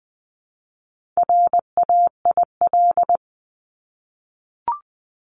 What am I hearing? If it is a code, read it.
Morse code, heard as a keyed tone, about 20 wpm, 700 Hz. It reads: RAIL
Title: 260 Lesson 42e - Words 20-20wpm